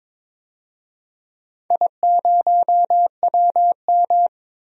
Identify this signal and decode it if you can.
Morse code, heard as a keyed tone, about 22 wpm, 700 Hz. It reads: I0WM
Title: DXCC-100 Call Signs - 4 Characters - 22wpm